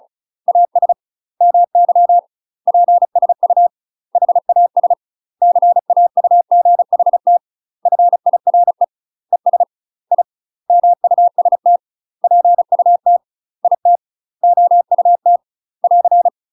Morse code, heard as a keyed tone, about 36 wpm, 700 Hz. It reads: AS MY PSU HAS CAUGHT FIRE ES I MUST PUT IT OUT <AR>
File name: UK MASTER QSOs TRAINING / 35WPM / UK MASTER QSOs 0009